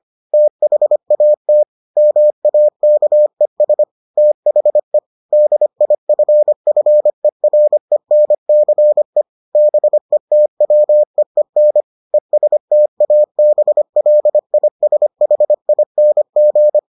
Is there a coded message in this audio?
Yes: Morse code from 0.3 to 16.8 s, heard as a keyed tone, about 25 wpm, 600 Hz. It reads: THAT MAKES THE DIFFERENCE BETWEEN ESTABLISHING